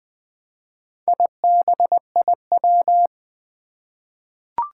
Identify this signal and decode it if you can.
Morse code, heard as a keyed tone, about 20 wpm, 700 Hz. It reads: IBIW
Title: Four Challenging Letter-Number Combinations 20wpm